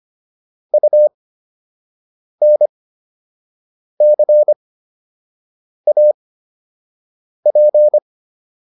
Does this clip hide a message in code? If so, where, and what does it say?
Yes: Morse code from 0.7 to 8.0 s, heard as a keyed tone, about 25 wpm, spaced as for 6 wpm, 600 Hz. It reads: UNCAP